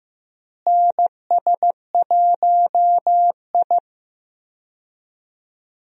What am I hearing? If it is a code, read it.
Morse code, heard as a keyed tone, about 15 wpm, 700 Hz. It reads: NS1I
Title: USA Call Signs - 4 Characters - Repeated 3x - 15wpm